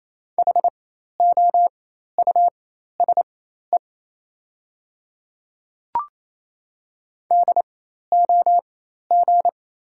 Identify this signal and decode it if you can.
Morse code, heard as a keyed tone, about 28 wpm, spaced as for 13 wpm, 700 Hz. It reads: HOUSE DOG
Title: 4x Character Spacing - Top 500 Words - 28wpm